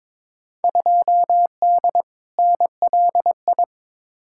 Morse code, heard as a keyed tone, about 22 wpm, 700 Hz. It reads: 2D NLI
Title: Field Day Exchanges Encore - 22wpm